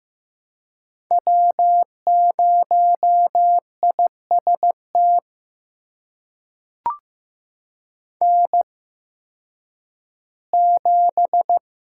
Morse code, heard as a keyed tone, about 15 wpm, spaced as for 10 wpm, 700 Hz. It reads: W0IST N7
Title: USA Call Signs - 5 Characters - 8x Character Spacing - 15wpm